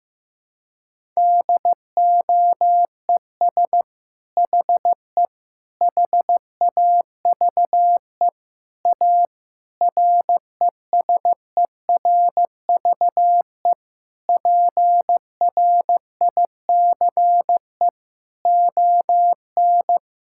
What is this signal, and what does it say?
Morse code, heard as a keyed tone, about 16 wpm, 700 Hz. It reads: DOES HE HAVE A RESERVE PRICE ON
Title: Sentences from Top 1000 Words - Review Part 6 - 15wpm